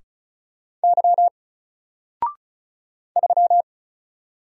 Morse code, heard as a keyed tone, about 36 wpm, 700 Hz. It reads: Y 3
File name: Single-Letter-Number-35wpm